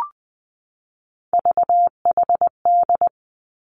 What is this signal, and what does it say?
Morse code, heard as a keyed tone, about 20 wpm, 700 Hz. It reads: VHD
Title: Three Challenging Letter-Number Combinations 20wpm